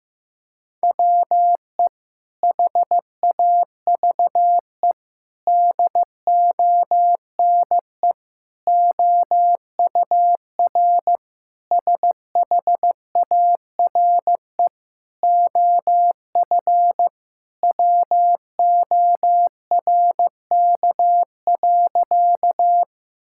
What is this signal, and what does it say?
Morse code, heard as a keyed tone, about 16 wpm, 700 Hz. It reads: WE HAVE DONE OUR SHARE OF WORK.
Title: Sentences from Top 1000 Words - Review Part 6 - 15wpm